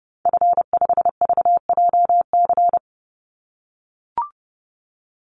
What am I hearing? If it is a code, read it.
Morse code, heard as a keyed tone, about 30 wpm, 700 Hz. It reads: F5VJC
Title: Call Signs - 30wpm